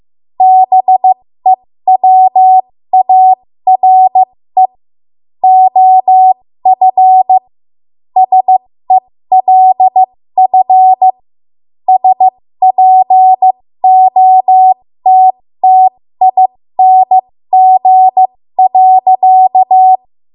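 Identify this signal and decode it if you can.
Morse code, heard as a keyed tone, about 15 wpm, spaced as for 13 wpm, 750 Hz. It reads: BEWARE OF SELF SPOTTING.
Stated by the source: 13 wpm Morse code practice archive transmission.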